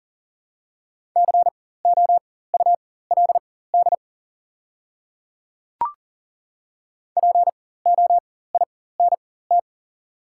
Morse code, heard as a keyed tone, about 40 wpm, spaced as for 19 wpm, 690 Hz. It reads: COULD POINT
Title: 4x Character Spacing - Top 200 Words - 40wpm